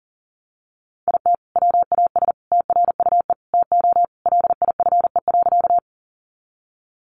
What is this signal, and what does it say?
Morse code, heard as a keyed tone, about 40 wpm, 700 Hz. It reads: IT WAS TRUE TO LIFE.
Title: Sentences from Top 400 Words 40wpm